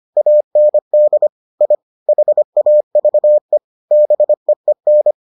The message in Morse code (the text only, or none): AND I HAVE BEEN